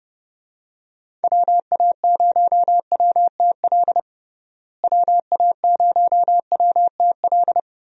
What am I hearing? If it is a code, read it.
Morse code, heard as a keyed tone, about 32 wpm, 700 Hz. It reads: WA0WTL WA0WTL
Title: USA Call Signs - 6 Characters - Repeated 2x - 30wpm